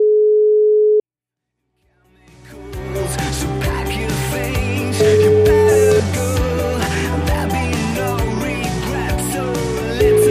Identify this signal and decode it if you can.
Ringback tone (Europe, 425 Hz), with noise.